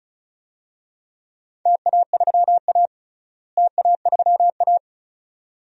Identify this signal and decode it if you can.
Morse code, heard as a keyed tone, about 36 wpm, 700 Hz. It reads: TA3A TA3A
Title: DXCC-100 Call Signs - 4 Characters - Repeated 2x - 35wpm